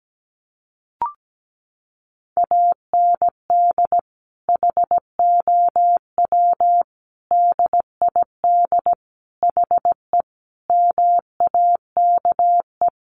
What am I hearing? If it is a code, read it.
Morse code, heard as a keyed tone, about 17 wpm, 700 Hz. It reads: AND HOW DID HE MAKE
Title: Sentences from Top 400 Words 17wpm